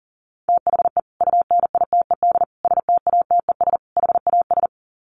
Morse code, heard as a keyed tone, about 40 wpm, 700 Hz. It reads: THE UNITED STATES HAS
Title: Sets of 4 Words 40wpm